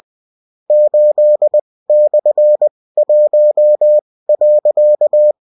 Morse code, heard as a keyed tone, about 20 wpm, spaced as for 16 wpm, 600 Hz. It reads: 8/1.